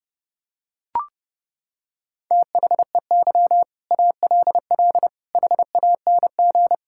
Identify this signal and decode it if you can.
Morse code, heard as a keyed tone, about 30 wpm, 700 Hz. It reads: THEY ALL HANG
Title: Sentences from Top 1600 Words 30wpm